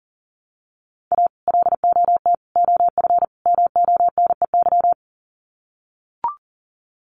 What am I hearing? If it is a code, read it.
Morse code, heard as a keyed tone, about 40 wpm, 700 Hz. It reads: A LOT OF MONEY
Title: Sets of 4 Words 40wpm